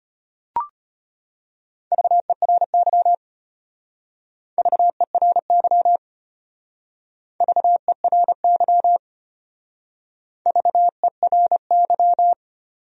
Morse code, heard as a keyed tone, about 31 wpm, 700 Hz. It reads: VERY VERY VERY VERY
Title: Speed-Racing Top 200 Words 25wpm